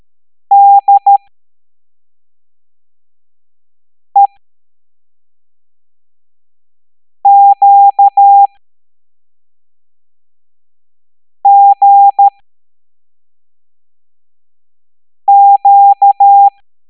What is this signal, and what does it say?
Morse code, heard as a keyed tone, about 13 wpm, spaced as for 3 wpm, 800 Hz. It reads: DEQGQ